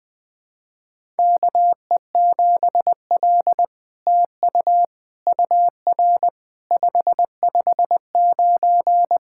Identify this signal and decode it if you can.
Morse code, heard as a keyed tone, about 20 wpm, 700 Hz. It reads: KE7L TU UR 559
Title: SOTA QSO Practice 20wpm